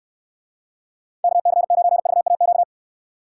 Morse code, heard as a keyed tone, about 64 wpm, 700 Hz. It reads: KC9VAQ